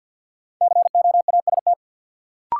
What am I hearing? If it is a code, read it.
Morse code, heard as a keyed tone, about 53 wpm, 700 Hz. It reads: COAST
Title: Sentences from Top 1300 Words - New Word Drill Alt - 50wpm